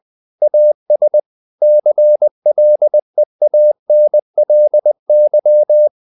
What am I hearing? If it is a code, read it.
Morse code, heard as a keyed tone, about 20 wpm, 600 Hz. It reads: AS CLEANLY
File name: the old man and the sea part 2 20wpm